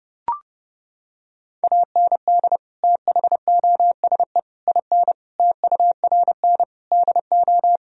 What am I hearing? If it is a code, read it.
Morse code, heard as a keyed tone, about 30 wpm, 700 Hz. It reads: AND THOSE IN TURN DO
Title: Sentences from Top 1000 Words - Review Part 1 - 30wpm